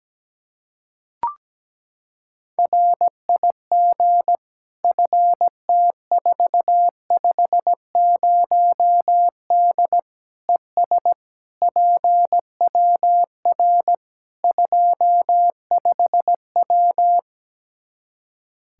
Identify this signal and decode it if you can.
Morse code, heard as a keyed tone, about 17 wpm, 700 Hz. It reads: RIG FT450D ES PWR 25W
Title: Practice QSO - Exchange 2 - Info - 17wpm